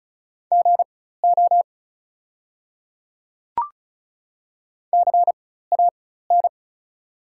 Morse code, heard as a keyed tone, about 36 wpm, spaced as for 16 wpm, 690 Hz. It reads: GO CAN
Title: 4x Character Spacing - Top 100 Words - 35wpm